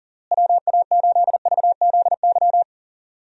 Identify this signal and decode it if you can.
Morse code, heard as a keyed tone, about 40 wpm, 700 Hz. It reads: WA8VZY
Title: USA Call Signs - 6 Characters - Repeated 3x - 40wpm